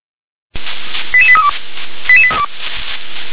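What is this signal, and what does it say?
Five-tone selective call (ZVEI), heard as quick runs of five tones, tones of about 70 ms. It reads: 88832
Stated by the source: ZVEI five-tone selective call received with an SDR.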